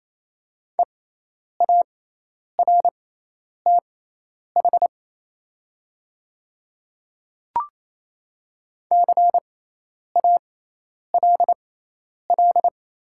Morse code, heard as a keyed tone, about 27 wpm, spaced as for 10 wpm, 700 Hz. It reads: EARTH CALL
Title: 6x Character Spacing - Top 200 Words - 28wpm